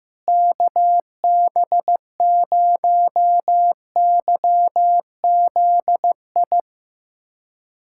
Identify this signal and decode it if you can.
Morse code, heard as a keyed tone, about 16 wpm, 700 Hz. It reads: KB0YZI
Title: USA Callsign DE Callsign 15wpm